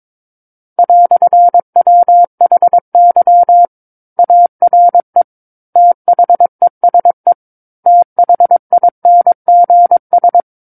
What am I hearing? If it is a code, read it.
Morse code, heard as a keyed tone, about 22 wpm, 700 Hz. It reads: "WHY ARE THESE THINGS
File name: Book One / 13-Chapter 13 - 22 WPM